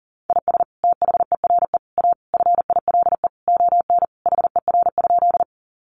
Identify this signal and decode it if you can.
Morse code, heard as a keyed tone, about 40 wpm, 700 Hz. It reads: IS THERE A FILE ON HER?
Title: Sentences from Top 1200 Words 40wpm